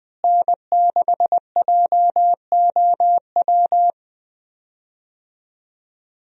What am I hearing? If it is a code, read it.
Morse code, heard as a keyed tone, about 21 wpm, 700 Hz. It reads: N6JOW
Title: USA Call Signs - 5 Characters - Repeated 2x - 20wpm